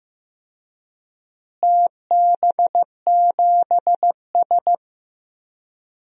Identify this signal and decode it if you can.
Morse code, heard as a keyed tone, about 15 wpm, 700 Hz. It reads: TB7S
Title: DXCC-100 Call Signs - 4 Characters - Repeated 3x - 15wpm